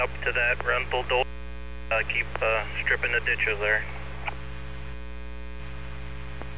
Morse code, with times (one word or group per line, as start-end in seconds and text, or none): none